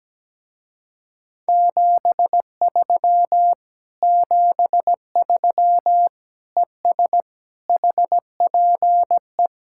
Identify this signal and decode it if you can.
Morse code, heard as a keyed tone, about 17 wpm, 700 Hz. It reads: 73 73 ES HPE
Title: Speed-Racing Common QSO Phrases - 17wpm